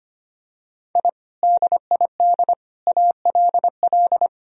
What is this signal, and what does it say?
Morse code, heard as a keyed tone, about 25 wpm, 700 Hz. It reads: I DID ALL
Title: Sentences from Top 1000 Words - Review Part 1 - 25wpm